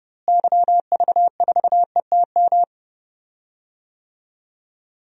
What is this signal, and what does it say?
Morse code, heard as a keyed tone, about 29 wpm, 700 Hz. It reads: YV4ETM
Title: DXCC-100 Call Signs - 6 Characters - Repeated 3x - 30wpm